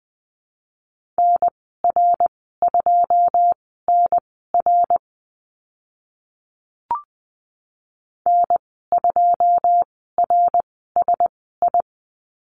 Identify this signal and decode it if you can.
Morse code, heard as a keyed tone, about 20 wpm, spaced as for 15 wpm, 700 Hz. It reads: NR2NR N2RSI
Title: 082 Lesson 14g - Number 2 - Call Signs 20wpm